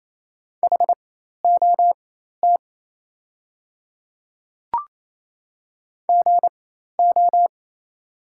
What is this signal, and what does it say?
Morse code, heard as a keyed tone, about 28 wpm, spaced as for 13 wpm, 700 Hz. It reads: HOT GO
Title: 4x Character Spacing - Top 100 Words - 28wpm